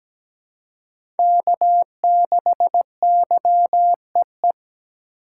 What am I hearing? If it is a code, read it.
Morse code, heard as a keyed tone, about 17 wpm, 700 Hz. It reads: K6YEE
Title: USA Call Signs - 2x Character Spacing - 17wpm